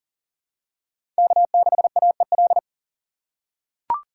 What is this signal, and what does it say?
Morse code, heard as a keyed tone, about 40 wpm, 700 Hz. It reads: K6AEL
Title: USA Call Signs - 5 Characters - 3x Character Spacing - 40wpm